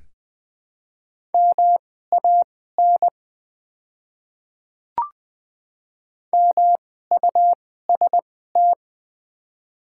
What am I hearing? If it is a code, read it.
Morse code, heard as a keyed tone, about 20 wpm, spaced as for 14 wpm, 700 Hz. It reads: MAN MUST
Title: 2x Character Spacing - Top 200 Words - 20wpm